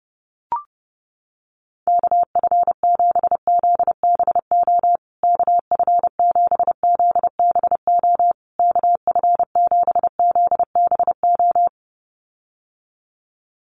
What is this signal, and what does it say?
Morse code, heard as a keyed tone, about 30 wpm, 700 Hz. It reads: KF7ZBO KF7ZBO KF7ZBO
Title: USA Call Signs Repeated 3x 30wpm